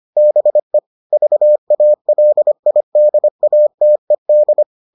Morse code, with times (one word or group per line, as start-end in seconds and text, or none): 0.2-0.8 BE
1.1-4.6 VALIDATED